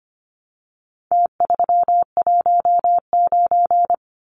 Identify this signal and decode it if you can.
Morse code, heard as a keyed tone, about 25 wpm, 700 Hz. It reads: T319
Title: Numbers - Contest Style - 4-Digit 25wpm